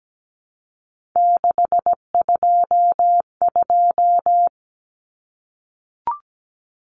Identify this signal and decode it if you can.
Morse code, heard as a keyed tone, about 17 wpm, 700 Hz. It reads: 622
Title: Common Cut Numbers - 3-Digit 17wpm